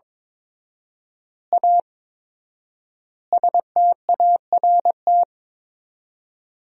Morse code, heard as a keyed tone, about 22 wpm, 700 Hz. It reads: A START
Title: 4x Word-Spacing - Sentences from Top 300 Words - 22wpm